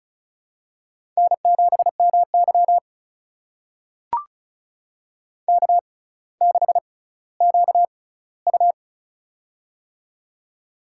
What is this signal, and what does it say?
Morse code, heard as a keyed tone, about 36 wpm, spaced as for 18 wpm, 700 Hz. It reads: N7MY K6QU
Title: USA Call Signs - 4 Characters - 6x Character Spacing - 35wpm